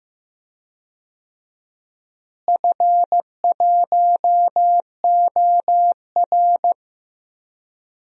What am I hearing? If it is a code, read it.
Morse code, heard as a keyed tone, about 15 wpm, 700 Hz. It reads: F1OR
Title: DXCC-100 Call Signs - 4 Characters - Repeated 2x - 15wpm